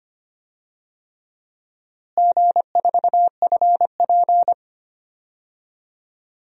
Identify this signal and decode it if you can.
Morse code, heard as a keyed tone, about 25 wpm, 700 Hz. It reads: G4FP
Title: DXCC-100 Call Signs - 4 Characters - Repeated 3x - 25wpm